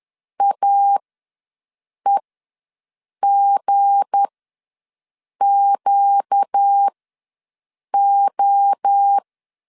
Morse code, heard as a keyed tone, about 10 wpm, spaced as for 6 wpm, 800 Hz. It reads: AEGQO
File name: LEZIONI / 7 GF9Z / 7 GF9 ws 10wpm 4034